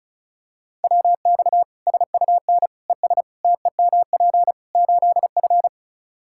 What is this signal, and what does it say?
Morse code, heard as a keyed tone, about 36 wpm, 700 Hz. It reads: WX SUN ES TEMP 8F
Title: Practice QSO - Exchange 2 - Info - 35wpm